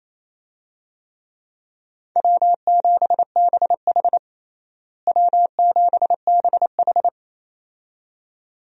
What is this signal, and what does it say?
Morse code, heard as a keyed tone, about 28 wpm, 700 Hz. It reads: W7BH W7BH
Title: USA Call Signs - 4 Characters - Repeated 2x - 28wpm